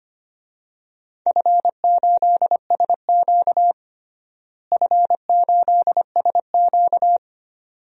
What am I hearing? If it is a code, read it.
Morse code, heard as a keyed tone, about 25 wpm, 700 Hz. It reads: F8SQ F8SQ
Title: DXCC-100 Call Signs - 4 Characters - Repeated 2x - 25wpm